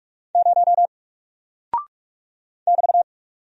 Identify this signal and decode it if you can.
Morse code, heard as a keyed tone, about 44 wpm, 700 Hz. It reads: 0 <BT>
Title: Single Character (Letter:Number:Punctuation:Prosign) 45wpm